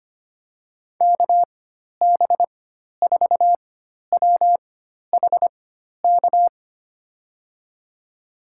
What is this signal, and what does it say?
Morse code, heard as a keyed tone, about 25 wpm, spaced as for 12 wpm, 700 Hz. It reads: KB4WHK
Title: USA Call Signs - 6 Characters - 4x Character Spacing - 25wpm